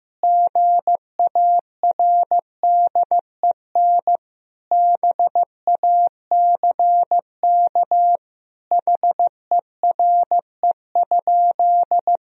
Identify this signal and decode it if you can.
Morse code, heard as a keyed tone, about 15 wpm, 700 Hz. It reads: GARDEN BACK HERE?
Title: Sentences from Top 1000 Words - Review Part 4 - 15wpm